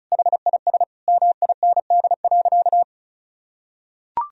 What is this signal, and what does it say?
Morse code, heard as a keyed tone, about 34 wpm, 700 Hz. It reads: HIS MIND.
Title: Sentences from Top 1000 Words - Review Part 2 - 35wpm